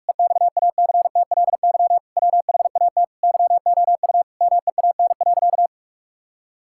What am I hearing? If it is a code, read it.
Morse code, heard as a keyed tone, about 46 wpm, 700 Hz. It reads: EXACTLY WHAT YOU MEAN.
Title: Sentences from Top 1000 Words - Review Part 6 - 45wpm